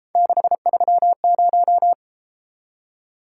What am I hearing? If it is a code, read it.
Morse code, heard as a keyed tone, about 31 wpm, 700 Hz. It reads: 630